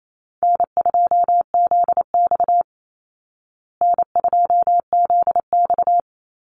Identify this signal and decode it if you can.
Morse code, heard as a keyed tone, about 28 wpm, 700 Hz. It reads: N2ZX N2ZX
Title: Call Signs Repeated 3x - 28wpm